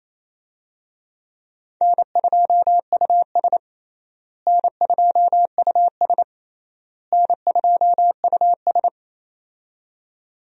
Morse code, heard as a keyed tone, about 28 wpm, 700 Hz. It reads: N2US N2US N2US
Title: USA Call Signs - 4 Characters - Repeated 3x - 28wpm